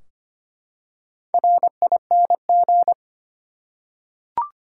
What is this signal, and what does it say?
Morse code, heard as a keyed tone, about 25 wpm, 700 Hz. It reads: RING
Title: Four Letter Words - 25wpm